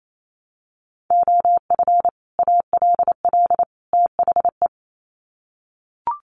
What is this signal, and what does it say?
Morse code, heard as a keyed tone, about 28 wpm, 700 Hz. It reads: OF ALL THE
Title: Sets of 3 Words 28wpm